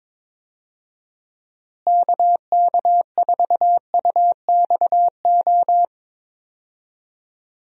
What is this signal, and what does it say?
Morse code, heard as a keyed tone, about 22 wpm, 700 Hz. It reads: KK4UXO
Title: USA Call Signs - 6 Characters - Repeated 3x - 22wpm